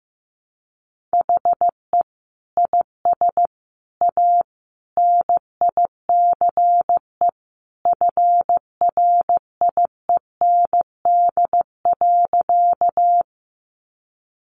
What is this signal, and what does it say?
Morse code, heard as a keyed tone, about 15 wpm, 700 Hz. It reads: HE IS A NICE FRIEND.
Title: Sentences from Top 1100 Words 15wpm